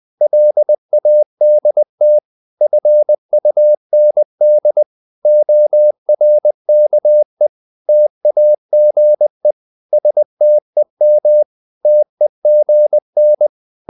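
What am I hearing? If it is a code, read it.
Morse code, heard as a keyed tone, about 20 wpm, 600 Hz. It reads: LADT FUND ORKE TAGE STEM TEGN
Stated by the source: Morse code practice of short Danish words at 20 wpm, from CW med Gnister.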